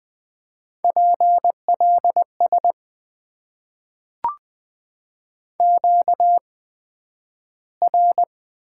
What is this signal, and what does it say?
Morse code, heard as a keyed tone, about 20 wpm, spaced as for 11 wpm, 700 Hz. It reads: PLS QR
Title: Top 90 QSO Elements - 8x Character Spacing - 20wpm